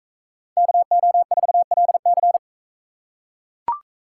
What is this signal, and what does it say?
Morse code, heard as a keyed tone, about 41 wpm, 700 Hz. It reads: KO4LC